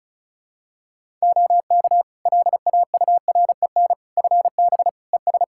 Morse code, heard as a keyed tone, about 34 wpm, 700 Hz. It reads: OK LAUREN FB ES
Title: Practice QSO - Exchange 2 - Info - 35wpm